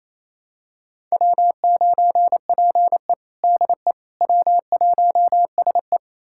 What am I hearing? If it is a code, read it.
Morse code, heard as a keyed tone, about 28 wpm, 700 Hz. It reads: W9PE DE W1SE
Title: USA Callsign DE Callsign - 4 Characters - 28wpm